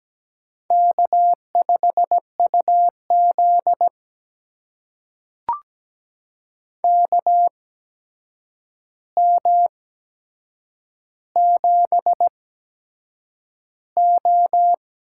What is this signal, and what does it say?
Morse code, heard as a keyed tone, about 17 wpm, spaced as for 7 wpm, 700 Hz. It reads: K5UZ KM7O
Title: USA Call Signs - 4 Characters - 8x Character Spacing - 17wpm